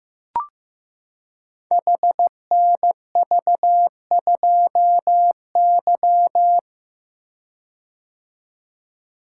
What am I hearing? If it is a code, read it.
Morse code, heard as a keyed tone, about 15 wpm, 700 Hz. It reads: HNV2Y
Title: Code Groups - 5 Letter:Numbers - 15wpm